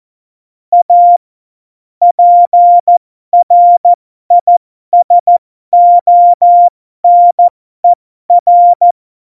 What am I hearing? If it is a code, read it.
Morse code, heard as a keyed tone, about 14 wpm, spaced as for 12 wpm, 700 Hz. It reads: A PRISONER
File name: Morsekurs / Witze